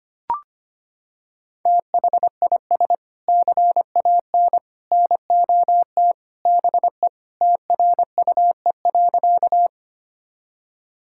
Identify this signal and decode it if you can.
Morse code, heard as a keyed tone, about 25 wpm, 700 Hz. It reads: THIS CAN NOT BE TRUE.
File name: practice / 25wpm / Sentences with 3-Word Phrases - 25wpm